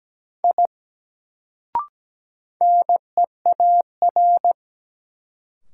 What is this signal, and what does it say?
Morse code, heard as a keyed tone, about 17 wpm, 700 Hz. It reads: I NEAR
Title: Sentences from Top 200 Words - New Word Drill - 17wpm